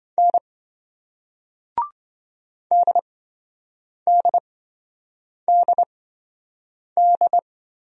Morse code, heard as a keyed tone, about 27 wpm, spaced as for 7 wpm, 700 Hz. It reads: N DDDD